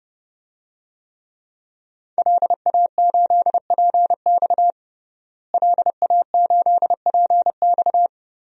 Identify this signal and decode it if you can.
Morse code, heard as a keyed tone, about 30 wpm, 700 Hz. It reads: LA8PX LA8PX
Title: DXCC-100 Call Signs - 5 Characters - Repeated 3x - 30wpm